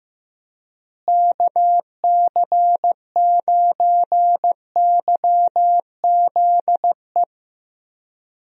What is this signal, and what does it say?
Morse code, heard as a keyed tone, about 15 wpm, 700 Hz. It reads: KC9YZE
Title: USA Call Signs - 6 Characters - Repeated 3x - 15wpm